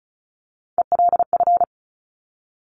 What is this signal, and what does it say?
Morse code, heard as a keyed tone, about 36 wpm, 690 Hz. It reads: ELF
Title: Top Three Letter Combinations in Words 35wpm